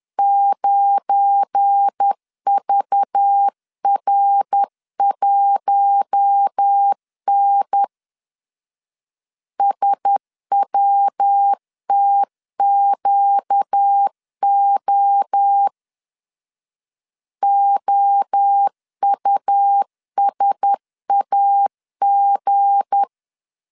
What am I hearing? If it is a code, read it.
Morse code, heard as a keyed tone, about 10 wpm, 800 Hz. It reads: 9VR1N SWTQO OUSAG
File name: CW / DET / 1 9 1500C S B 10wpm 4439